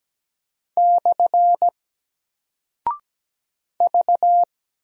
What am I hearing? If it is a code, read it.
Morse code, heard as a keyed tone, about 17 wpm, 700 Hz. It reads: / V
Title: Single Character (Letter:Number:Punctuation:Prosign) 17wpm